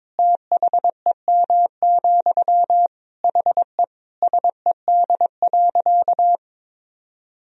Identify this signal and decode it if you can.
Morse code, heard as a keyed tone, about 22 wpm, 700 Hz. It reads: THEM, HE SED.
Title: Sentences from Top 400 Words and Abbreviations 22wpm